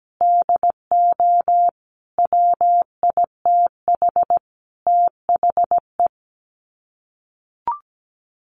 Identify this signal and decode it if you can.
Morse code, heard as a keyed tone, about 17 wpm, 700 Hz. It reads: DO WITH THE
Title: Sets of 5 Words Encore 17wpm